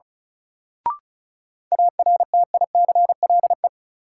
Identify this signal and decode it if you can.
Morse code, heard as a keyed tone, about 36 wpm, 700 Hz. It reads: ARTICLE
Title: Sentences from Top 1000 Words - Review Part 5 - New Word Drill - 35wpm